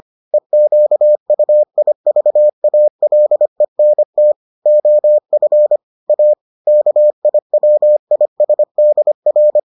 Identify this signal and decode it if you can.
Morse code, heard as a keyed tone, about 25 wpm, 600 Hz. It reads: EQUIVALENT OF A KIWISDR